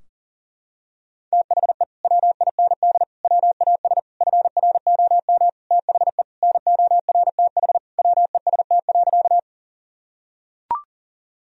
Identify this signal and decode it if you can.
Morse code, heard as a keyed tone, about 37 wpm, 700 Hz. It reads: THE WIND WAS FROM THE NORTH WEST.
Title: Sentences from Top 1000 Words - Review Part 2 - 40wpm